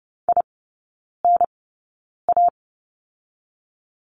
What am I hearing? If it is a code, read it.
Morse code, heard as a keyed tone, about 30 wpm, spaced as for 9 wpm, 690 Hz. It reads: INA
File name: Lesson 17e - Letter u - Three Characters 30wpm